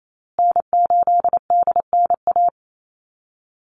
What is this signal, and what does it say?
Morse code, heard as a keyed tone, about 28 wpm, 700 Hz. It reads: N8DNA
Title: Call Signs Repeated 3x - 28wpm